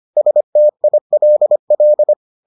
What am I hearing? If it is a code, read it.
Morse code, heard as a keyed tone, about 25 wpm, 600 Hz. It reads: STILL